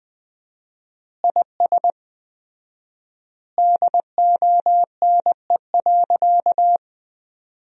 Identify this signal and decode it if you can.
Morse code, heard as a keyed tone, about 20 wpm, 700 Hz. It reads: IS DONE.
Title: 4x Word-Spacing - Sentences from Top 500 Words - 20wpm